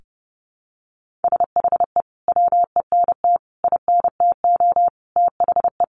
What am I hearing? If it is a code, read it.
Morse code, heard as a keyed tone, about 30 wpm, 700 Hz. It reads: SHE WENT INTO THE
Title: Sentences from Top 1400 Words 30wpm